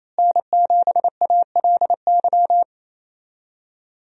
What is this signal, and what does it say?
Morse code, heard as a keyed tone, about 28 wpm, 700 Hz. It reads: N7ALY
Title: USA Call Signs - 5 Characters - Repeated 2x - 28wpm